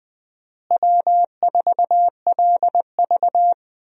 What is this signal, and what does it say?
Morse code, heard as a keyed tone, about 20 wpm, 700 Hz. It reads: W4LV
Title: USA Call Signs - 4 Characters - Repeated 2x - 20wpm